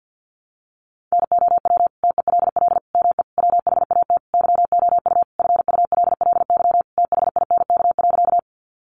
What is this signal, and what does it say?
Morse code, heard as a keyed tone, about 50 wpm, 700 Hz. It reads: NOW TELL ME WHAT YOU FULLY THINK.
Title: Sentences from Top 1300 Words 50wpm